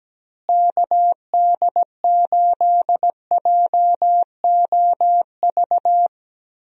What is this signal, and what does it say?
Morse code, heard as a keyed tone, about 17 wpm, 700 Hz. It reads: KD8JOV
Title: USA Call Signs - 6 Characters - Repeated 3x - 17wpm